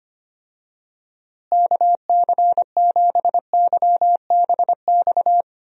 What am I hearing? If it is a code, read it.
Morse code, heard as a keyed tone, about 25 wpm, 700 Hz. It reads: KC7YBX
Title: USA Call Signs - 6 Characters - Repeated 3x - 25wpm